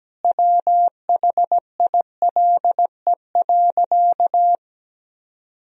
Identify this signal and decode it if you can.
Morse code, heard as a keyed tone, about 17 wpm, 700 Hz. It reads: WHILE.
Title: Sentences from Top 1000 Words - Review Part 2 - 17wpm